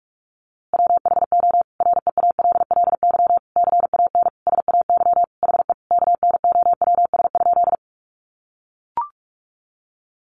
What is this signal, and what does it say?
Morse code, heard as a keyed tone, about 44 wpm, 700 Hz. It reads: WHO REALLY CAN SAY HE KNOWS?
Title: A Sound Of Thunder 45wpm